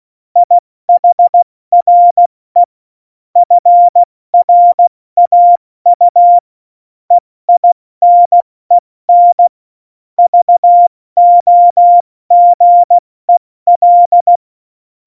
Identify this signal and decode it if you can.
Morse code, heard as a keyed tone, about 16 wpm, 700 Hz. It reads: IHRE FRAU EINEN VOGEL
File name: Morsekurs / Witze / Witz 13 March 2018, 70 F80